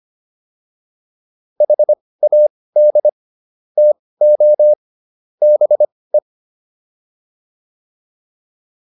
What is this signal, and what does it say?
Morse code, heard as a keyed tone, about 25 wpm, spaced as for 18 wpm, 600 Hz. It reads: HAD TO BE